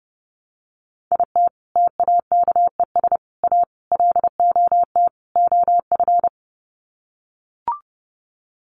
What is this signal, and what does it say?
Morse code, heard as a keyed tone, about 30 wpm, 700 Hz. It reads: IT TAKES A LOT OF
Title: Sets of 5 Words Encore 30wpm